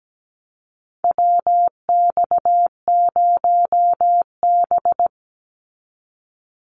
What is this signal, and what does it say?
Morse code, heard as a keyed tone, about 17 wpm, 700 Hz. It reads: WX0B
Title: Call Signs - 17wpm